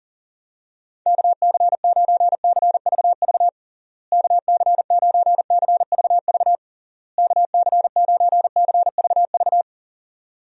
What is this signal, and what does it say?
Morse code, heard as a keyed tone, about 37 wpm, 700 Hz. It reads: KC9CVV KC9CVV KC9CVV
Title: USA Call Signs - 6 Characters - Repeated 3x - 40wpm